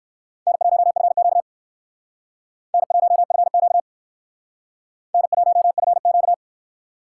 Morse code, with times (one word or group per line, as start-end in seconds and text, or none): 0.5-1.4 N1FQ
2.7-3.8 N1FQ
5.1-6.3 N1FQ